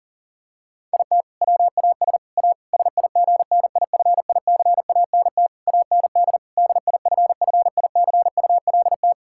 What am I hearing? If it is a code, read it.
Morse code, heard as a keyed tone, about 40 wpm, 700 Hz. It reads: IT WAS A SIGNIFICANT AND DIFFICULT
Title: Sentences from Top 1000 Words - Review Part 6 - 40wpm